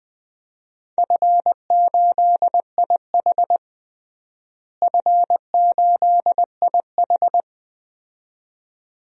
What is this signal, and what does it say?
Morse code, heard as a keyed tone, about 20 wpm, 700 Hz. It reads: F8IH F8IH
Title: DXCC-100 Call Signs - 4 Characters - Repeated 2x - 20wpm